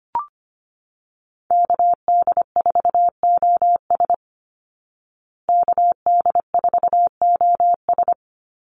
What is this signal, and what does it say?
Morse code, heard as a keyed tone, about 25 wpm, 700 Hz. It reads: KD4OS KD4OS
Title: Call Signs Repeated 3x - 25wpm